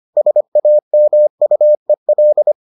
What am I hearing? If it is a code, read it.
Morse code, heard as a keyed tone, about 25 wpm, 600 Hz. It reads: SAMUEL